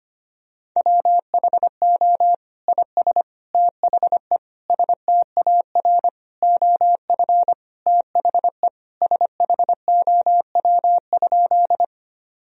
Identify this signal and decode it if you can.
Morse code, heard as a keyed tone, about 25 wpm, 700 Hz. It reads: WHO IS THE STAR OF THE SHOW?
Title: Sentences from Top 1000 Words - Review Part 2 - 25wpm